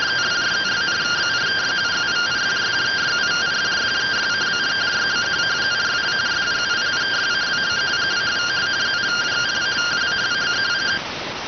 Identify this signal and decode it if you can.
Two-tone teleprinter signal, about 170 Hz shift.